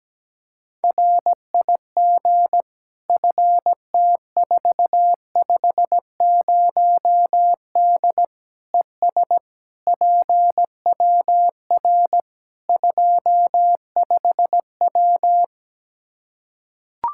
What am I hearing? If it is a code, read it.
Morse code, heard as a keyed tone, about 17 wpm, 700 Hz. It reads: RIG FT450D ES PWR 25W
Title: K1USN SST QSO Practice 17wpm